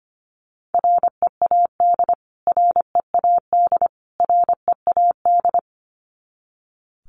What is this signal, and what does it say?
Morse code, heard as a keyed tone, about 25 wpm, 700 Hz. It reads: READ READ READ
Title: Top 300 Words Repeated 25wpm